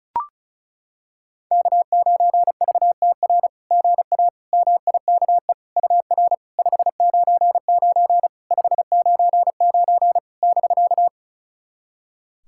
Morse code, heard as a keyed tone, about 36 wpm, 690 Hz. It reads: K9VTR GA MIKE UR 599 599 <BK>
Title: SOTA QSO Practice 35wpm